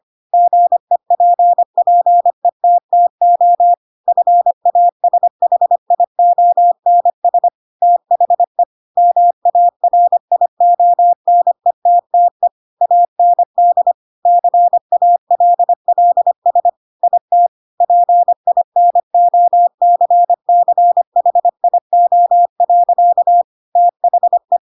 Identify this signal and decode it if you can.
Morse code, heard as a keyed tone, about 25 wpm, 700 Hz. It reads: GEPPETTO FASHIONS THE MARIONETTE AND CALLS IT PINOCCHIO. THE
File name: Pinocchio / Pinocchio ch3 25 25 ik1hge